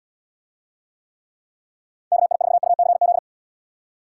Morse code, heard as a keyed tone, about 78 wpm, 700 Hz. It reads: KI4UCZ